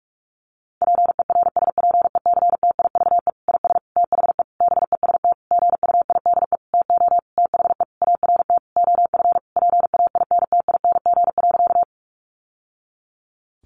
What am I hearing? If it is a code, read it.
Morse code, heard as a keyed tone, about 44 wpm, 700 Hz. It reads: PERSPECTIVE IS THE BEST GUIDE TO THE ART OF PAINTING.
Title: Sentences from Top 1200 Words 45wpm